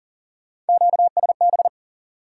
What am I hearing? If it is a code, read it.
Morse code, heard as a keyed tone, about 52 wpm, 700 Hz. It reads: QSB